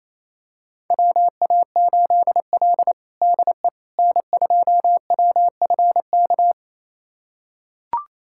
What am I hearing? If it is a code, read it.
Morse code, heard as a keyed tone, about 28 wpm, 700 Hz. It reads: WA8L DE N2WFK
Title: USA Callsign DE Callsign 28wpm